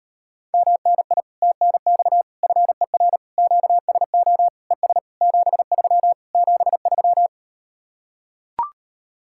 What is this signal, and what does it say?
Morse code, heard as a keyed tone, about 38 wpm, 700 Hz. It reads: MNI TNX FER QSO ES 73 73